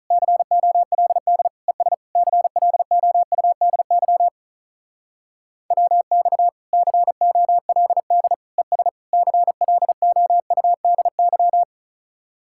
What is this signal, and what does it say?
Morse code, heard as a keyed tone, about 44 wpm, 700 Hz. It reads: COLD ES CLOUDY WX COLD ES CLOUDY